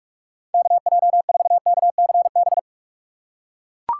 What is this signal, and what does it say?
Morse code, heard as a keyed tone, about 41 wpm, 700 Hz. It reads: KJ4KCB
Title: USA Call Signs - 6 Characters - 6x Character Spacing - 45wpm